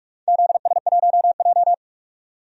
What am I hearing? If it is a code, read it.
Morse code, heard as a keyed tone, about 46 wpm, 700 Hz. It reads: ZS1J
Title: DXCC-100 Call Signs - 4 Characters - 45wpm